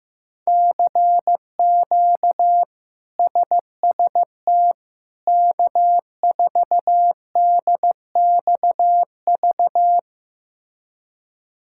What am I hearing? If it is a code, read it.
Morse code, heard as a keyed tone, about 15 wpm, 700 Hz. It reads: CQ SST K4DXV
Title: K1USN SST QSO Practice - 15wpm